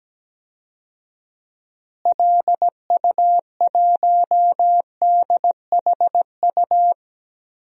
Morse code, heard as a keyed tone, about 17 wpm, 700 Hz. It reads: LU1DHU
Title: DXCC-100 Call Signs - 6 Characters - Repeated 2x - 17wpm